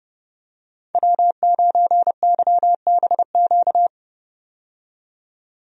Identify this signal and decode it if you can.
Morse code, heard as a keyed tone, about 30 wpm, 700 Hz. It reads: W9YBQ
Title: USA Call Signs - 5 Characters - 30wpm